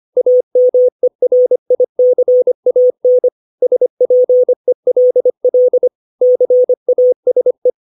Morse code, heard as a keyed tone, about 25 wpm, 500 Hz. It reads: AMERICAN SPELL CASE
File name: 632 words / 632 Words 25WPM